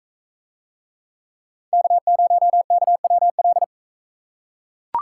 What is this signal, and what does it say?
Morse code, heard as a keyed tone, about 41 wpm, 700 Hz. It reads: K0KWL